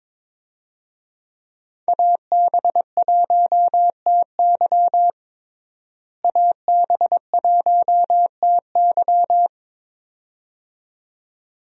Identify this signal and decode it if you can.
Morse code, heard as a keyed tone, about 22 wpm, 700 Hz. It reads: AB1TY AB1TY
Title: USA Call Signs - 5 Characters - Repeated 2x - 22wpm